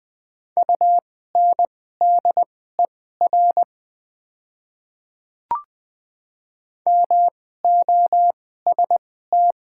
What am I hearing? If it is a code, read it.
Morse code, heard as a keyed tone, about 20 wpm, spaced as for 15 wpm, 700 Hz. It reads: UNDER MOST
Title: 2x Character Spacing - Top 200 Words - 20wpm